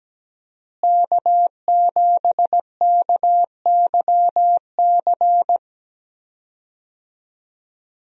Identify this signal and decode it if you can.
Morse code, heard as a keyed tone, about 17 wpm, 700 Hz. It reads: K7KYC
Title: USA Call Signs - 5 Characters - Repeated 2x - 17wpm